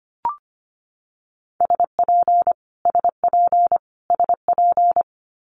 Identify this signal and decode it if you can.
Morse code, heard as a keyed tone, about 25 wpm, 700 Hz. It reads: SP SP SP
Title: Multi-letter Phonemes Repeated 3x 25wpm